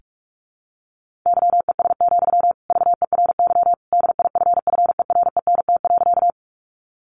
Morse code, heard as a keyed tone, about 46 wpm, 700 Hz. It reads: YES, VERY DIFFERENT.
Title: Sentences from Top 600 Words 45wpm